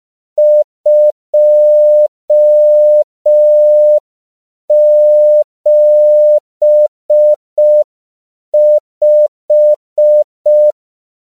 Morse code, heard as a keyed tone, about 5 wpm, 600 Hz. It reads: 275
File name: CW / 5PPM